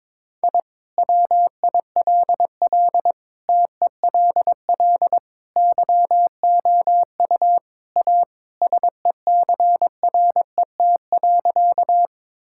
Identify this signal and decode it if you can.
Morse code, heard as a keyed tone, about 22 wpm, 700 Hz. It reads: I WILL TELL YOU A SECRET.
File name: practice / 22wpm / Sentences with 2-Word Phrases - 22wpm